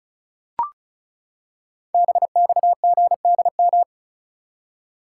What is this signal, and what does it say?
Morse code, heard as a keyed tone, about 36 wpm, 690 Hz. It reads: BXGDM
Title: Code Groups - 5 Letters - 35wpm